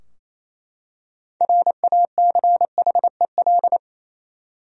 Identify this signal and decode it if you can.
Morse code, heard as a keyed tone, about 28 wpm, 700 Hz. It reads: RACHEL
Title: USA Common Names - 28wpm